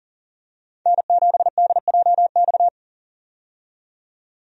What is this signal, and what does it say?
Morse code, heard as a keyed tone, about 40 wpm, 700 Hz. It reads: N7DJX
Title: USA Call Signs - 5 Characters - 40wpm